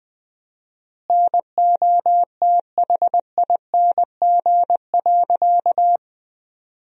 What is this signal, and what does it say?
Morse code, heard as a keyed tone, about 20 wpm, 700 Hz. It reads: NOTHING.
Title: 4x Word-Spacing - Sentences from Top 500 Words - 20wpm